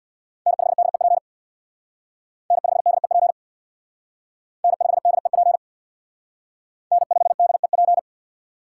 Morse code, heard as a keyed tone, about 64 wpm, 700 Hz. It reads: N5DEP N5DEP N5DEP N5DEP